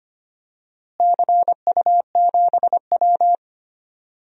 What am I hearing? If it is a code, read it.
Morse code, heard as a keyed tone, about 25 wpm, 700 Hz. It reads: CU7W
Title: DXCC-100 Call Signs - 4 Characters - Repeated 3x - 25wpm